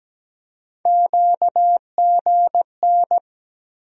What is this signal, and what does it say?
Morse code, heard as a keyed tone, about 17 wpm, 700 Hz. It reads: QGN
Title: USA Call Sign Postfixes - 3 Letters 17wpm